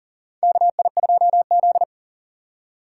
Morse code, heard as a keyed tone, about 40 wpm, 700 Hz. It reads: KI2Z
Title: USA Call Signs - 4 Characters - 6x Character Spacing - 40wpm